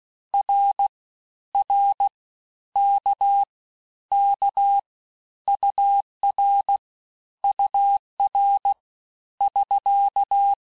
Morse code, heard as a keyed tone, about 16 wpm, 800 Hz. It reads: R R K K UR UR <SK>
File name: cours-du-lundi files / cours-1592--teste-abreviations-vitesse-15mpm-puis-20mpm